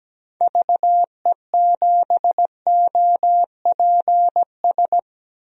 Morse code, heard as a keyed tone, about 17 wpm, 700 Hz. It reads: VE7OPS
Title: DXCC-100 Call Signs - 6 Characters - 17wpm